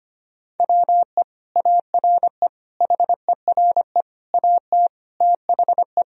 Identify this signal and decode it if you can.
Morse code, heard as a keyed tone, about 25 wpm, 700 Hz. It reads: WE ARE HERE AT THE
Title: Sentences from Top 1000 Words - Review Part 1 - 25wpm